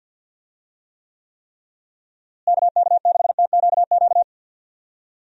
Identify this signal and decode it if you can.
Morse code, heard as a keyed tone, about 48 wpm, 700 Hz. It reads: KK6TQQ
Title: USA Call Signs - 6 Characters - Repeated 3x - 50wpm